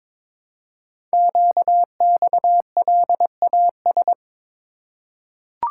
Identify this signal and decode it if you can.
Morse code, heard as a keyed tone, about 22 wpm, 700 Hz. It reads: QXLAS
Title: Code Groups - 5 Letters - 22wpm